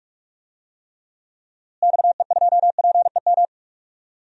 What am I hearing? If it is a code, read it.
Morse code, heard as a keyed tone, about 46 wpm, 700 Hz. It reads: XE2PEM
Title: DXCC-100 Call Signs - 6 Characters - Repeated 3x - 45wpm